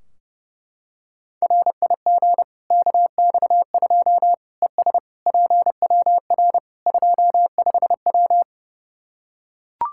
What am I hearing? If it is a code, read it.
Morse code, heard as a keyed tone, about 30 wpm, 700 Hz. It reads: RIG KX2 ES PWR 25W
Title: Practice QSO - Exchange 2 - Info - 30wpm